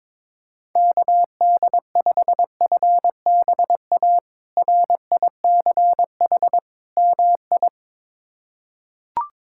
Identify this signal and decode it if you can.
Morse code, heard as a keyed tone, about 22 wpm, 700 Hz. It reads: KD5FBA RICH MI
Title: K1USN SST QSO Practice - 22wpm